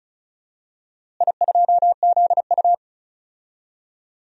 Morse code, heard as a keyed tone, about 34 wpm, 700 Hz. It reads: I2ZU
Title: DXCC-100 Call Signs - 4 Characters - Repeated 3x - 35wpm